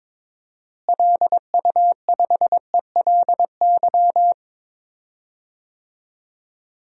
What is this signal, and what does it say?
Morse code, heard as a keyed tone, about 22 wpm, 700 Hz. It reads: LU5ELY
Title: DXCC-100 Call Signs - 6 Characters - Repeated 2x - 22wpm